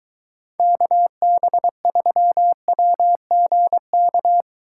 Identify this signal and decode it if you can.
Morse code, heard as a keyed tone, about 22 wpm, 700 Hz. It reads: KB3WGK